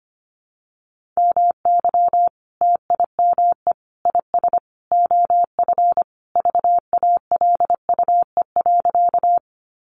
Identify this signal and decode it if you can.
Morse code, heard as a keyed tone, about 25 wpm, 700 Hz. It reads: MY TIME IS OF VALUE.
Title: Sentences from Top 600 Words 25wpm